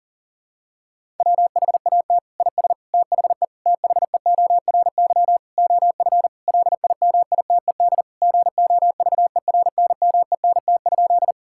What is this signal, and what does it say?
Morse code, heard as a keyed tone, about 40 wpm, 690 Hz. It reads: WHAT IS THE THEORY OF LIMITED GOVERNMENT?
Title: Sentences from Top 1000 Words - Review Part 6 - 40wpm